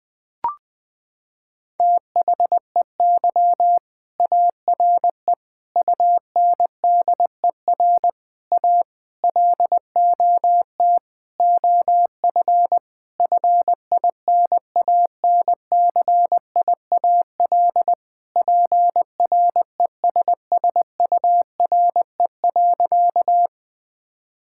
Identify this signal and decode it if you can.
Morse code, heard as a keyed tone, about 20 wpm, 700 Hz. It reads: THEY ARE UNDER A LOT OF FINANCIAL PRESSURE.
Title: Sentences from Top 1000 Words - Review Part 6 - 20wpm